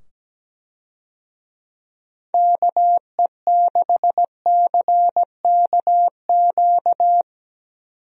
Morse code, heard as a keyed tone, about 17 wpm, 700 Hz. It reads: KE6CKQ
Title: USA Call Signs - 6 Characters - 17wpm